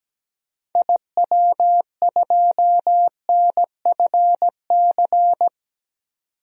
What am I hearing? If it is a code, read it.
Morse code, heard as a keyed tone, about 17 wpm, 700 Hz. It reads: IW2NFC
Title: DXCC-100 Call Signs - 6 Characters - 17wpm